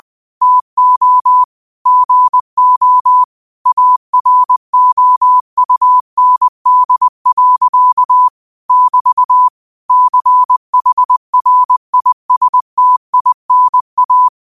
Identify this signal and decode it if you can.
Morse code, heard as a keyed tone, about 20 wpm, 1000 Hz. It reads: TO GO AROUND. <BT> CHRISTINA
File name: qotdfiles / 20wpm